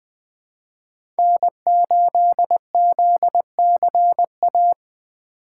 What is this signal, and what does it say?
Morse code, heard as a keyed tone, about 20 wpm, 700 Hz. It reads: N8ZCA
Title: USA Call Signs - 5 Characters - 4x Character Spacing - 20wpm